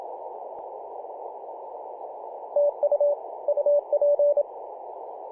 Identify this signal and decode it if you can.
Morse code, heard as a keyed tone, about 27 wpm, 580 Hz. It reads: TU UP
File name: PJ2/G3TXF 18.086MHz CW